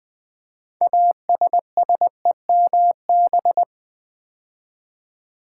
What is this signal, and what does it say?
Morse code, heard as a keyed tone, about 20 wpm, 700 Hz. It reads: ASSEMB
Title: Build A Word - 2 Letters At A Time - 8-Letter Words - 20wpm